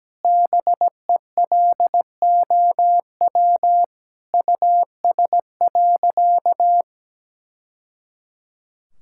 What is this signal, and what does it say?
Morse code, heard as a keyed tone, about 17 wpm, 700 Hz. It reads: BELOW US.
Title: Sentences from Top 1000 Words - Review Part 5 - 17wpm